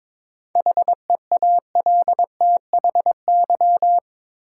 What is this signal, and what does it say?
Morse code, heard as a keyed tone, about 21 wpm, 700 Hz. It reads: HEALTHY
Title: Sentences from Top 1000 Words - Review Part 5 - New Word Drill - 22wpm